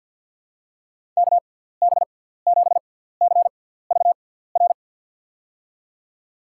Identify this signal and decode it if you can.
Morse code, heard as a keyed tone, about 46 wpm, spaced as for 17 wpm, 700 Hz. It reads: KB7CVR
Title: USA Call Signs - 6 Characters - 6x Character Spacing - 50wpm